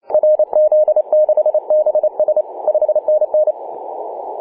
Morse code, heard as a keyed tone, about 30 wpm, 610 Hz. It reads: RZ6BS 5NN
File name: posts / oct-19-0756z-vk9xsp-24-901mhz-cw